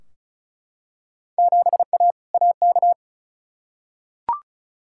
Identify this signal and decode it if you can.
Morse code, heard as a keyed tone, about 36 wpm, 700 Hz. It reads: 7A AK
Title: Field Day Exchanges Encore - 35wpm